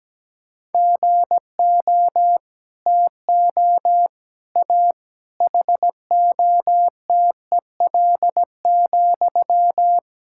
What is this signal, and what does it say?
Morse code, heard as a keyed tone, about 17 wpm, 700 Hz. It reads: GO TO A HOTEL,
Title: Sentences from Top 1600 Words 17wpm